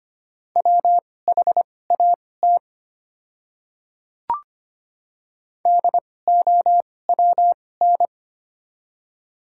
Morse code, heard as a keyed tone, about 25 wpm, spaced as for 18 wpm, 700 Hz. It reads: WHAT DOWN
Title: 2x Character Spacing - Top 300 Words - 25wpm